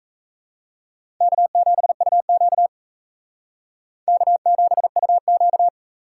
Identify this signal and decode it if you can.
Morse code, heard as a keyed tone, about 36 wpm, 700 Hz. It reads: K7UQ K7UQ